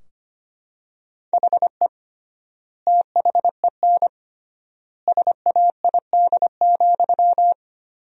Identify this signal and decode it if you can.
Morse code, heard as a keyed tone, about 25 wpm, 700 Hz. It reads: HE THEN SAID,
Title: 3x Word-Spacing - Sentences from Top 300 Words - 25wpm